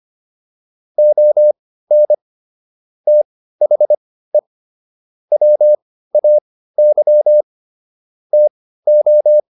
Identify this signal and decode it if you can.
Morse code, heard as a keyed tone, about 25 wpm, spaced as for 15 wpm, 600 Hz. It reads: ON THE WAY TO